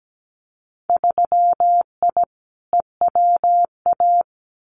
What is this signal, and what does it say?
Morse code, heard as a keyed tone, about 17 wpm, 700 Hz. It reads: 3I EWA
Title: Winter Field Day Exchanges Encore 17wpm